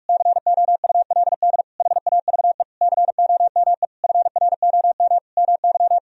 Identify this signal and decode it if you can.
Morse code, heard as a keyed tone, about 46 wpm, 700 Hz. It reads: COULD HAVE COME FROM MY
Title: Sentences from Top 1000 Words - Review Part 4 - 45wpm